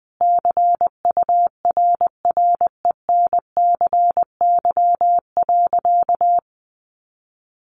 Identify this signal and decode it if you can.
Morse code, heard as a keyed tone, about 20 wpm, 700 Hz. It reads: CURRENCY.
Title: Sentences from Top 1500 Words 20wpm